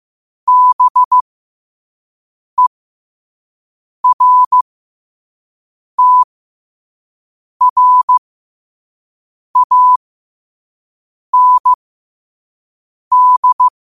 Morse code, heard as a keyed tone, about 15 wpm, spaced as for 5 wpm, 1000 Hz. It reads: BERTRAND
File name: qotdfiles / 5wpm